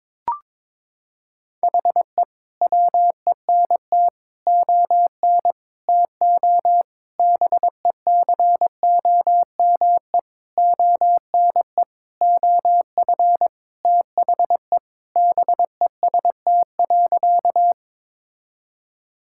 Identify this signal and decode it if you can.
Morse code, heard as a keyed tone, about 21 wpm, 700 Hz. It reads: HE WENT ON TO BECOME ONE OF THE BEST.
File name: Sentences with 7-Word Phrases - 22wpm